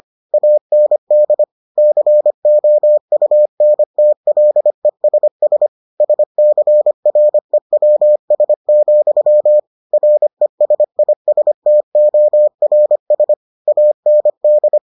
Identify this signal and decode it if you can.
Morse code, heard as a keyed tone, about 25 wpm, 600 Hz. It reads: AND COUNTLESS SCREWS, RESISTORS AND